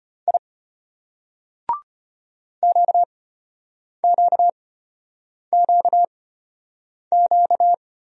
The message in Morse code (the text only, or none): I QQQQ